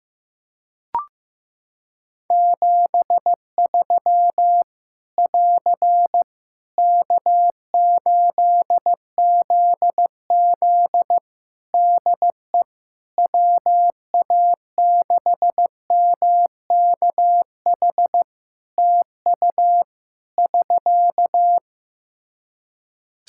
Morse code, heard as a keyed tone, about 15 wpm, 700 Hz. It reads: 73 <AR> K8ZZ DE WA6MKH TU <SK>
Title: K1USN SST QSO Practice 15wpm